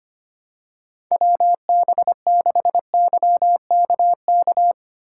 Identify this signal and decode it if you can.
Morse code, heard as a keyed tone, about 25 wpm, 700 Hz. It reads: WB6YKK
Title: USA Call Signs - 2x Character Spacing - 25wpm